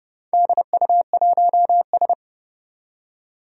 Morse code, heard as a keyed tone, about 30 wpm, 700 Hz. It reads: DU1S
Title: DXCC-100 Call Signs - 4 Characters - 30wpm